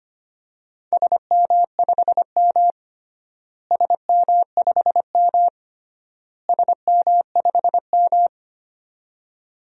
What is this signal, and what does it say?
Morse code, heard as a keyed tone, about 25 wpm, 700 Hz. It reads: SM5M SM5M SM5M
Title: DXCC-100 Call Signs - 4 Characters - Repeated 3x - 25wpm